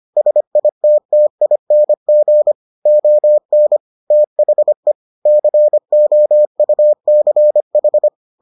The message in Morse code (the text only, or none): SITTING ON THE COUCH